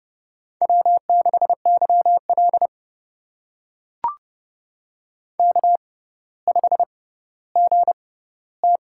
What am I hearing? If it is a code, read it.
Morse code, heard as a keyed tone, about 31 wpm, spaced as for 16 wpm, 700 Hz. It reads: W6YL K5GT
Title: USA Call Signs - 4 Characters - 6x Character Spacing - 30wpm